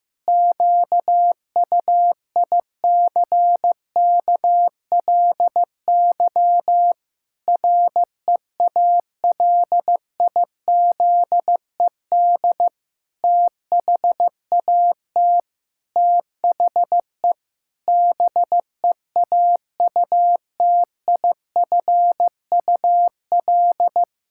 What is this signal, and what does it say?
Morse code, heard as a keyed tone, about 15 wpm, 700 Hz. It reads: QUICKLY REALIZED THAT THE BEAUTIFUL
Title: Pangram Challenge v2 - 15wpm